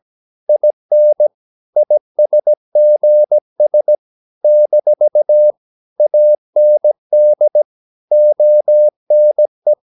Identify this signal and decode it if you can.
Morse code, heard as a keyed tone, about 17 wpm, 600 Hz. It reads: IN ISGS - AND ONE
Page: Romanen Never af Ken Follet på engelsk.